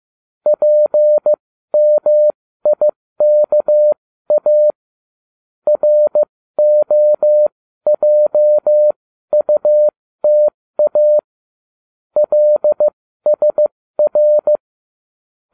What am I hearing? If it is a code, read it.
Morse code, heard as a keyed tone, about 15 wpm, spaced as for 13 wpm, 600 Hz. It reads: PMIKA ROJUTA LSR